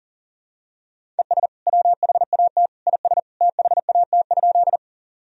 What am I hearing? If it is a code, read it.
Morse code, heard as a keyed tone, about 40 wpm, 700 Hz. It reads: ES WHAT IS THAT?
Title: Sentences from Top 100 Words and Abbreviations 40wpm